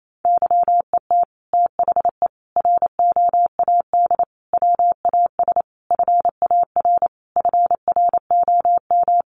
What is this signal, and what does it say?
Morse code, heard as a keyed tone, about 28 wpm, 700 Hz. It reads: YET THE ROAD WAS FAR FROM
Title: Sentences from Top 1400 Words 28wpm